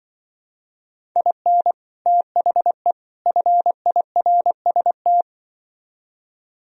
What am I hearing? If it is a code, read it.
Morse code, heard as a keyed tone, about 24 wpm, 700 Hz. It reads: IN THE FIRST
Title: Speed-Racing - Sets of 3 Words 20wpm